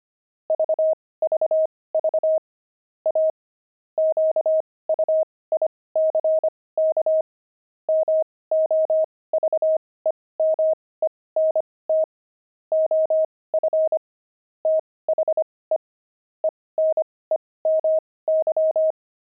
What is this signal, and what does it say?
Morse code, heard as a keyed tone, about 25 wpm, spaced as for 18 wpm, 630 Hz. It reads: VVV A QUICK MOVEMENT OF THE ENEMY
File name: pangram-fs-18-cs-25-enemy